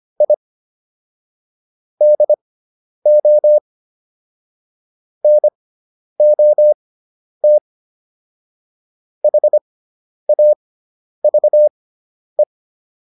Morse code, heard as a keyed tone, about 25 wpm, spaced as for 10 wpm, 600 Hz. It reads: I DO NOT HAVE